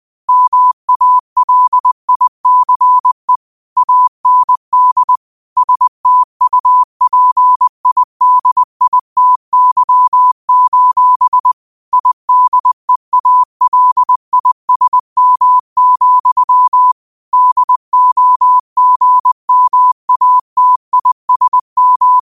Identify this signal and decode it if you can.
Morse code, heard as a keyed tone, about 20 wpm, 1000 Hz. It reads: MALICE AND STUPIDITY: IDEALISM, DOGMATISM